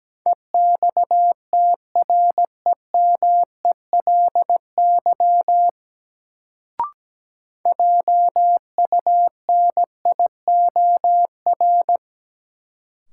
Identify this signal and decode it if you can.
Morse code, heard as a keyed tone, about 17 wpm, 700 Hz. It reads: EXTREMELY JUNIOR
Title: Sentences from Top 1000 Words - Review Part 7 - New Word Drill - 17wpm